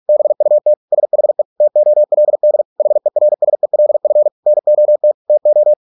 Morse code, heard as a keyed tone, about 46 wpm, 600 Hz. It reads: BUT SHE TOLD HERSELF NOT TO